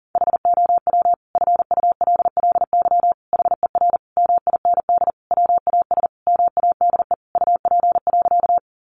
Morse code, heard as a keyed tone, about 40 wpm, 700 Hz. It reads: HOW FULLY HER MIND WAS MADE UP.
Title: Sentences from Top 1300 Words 40wpm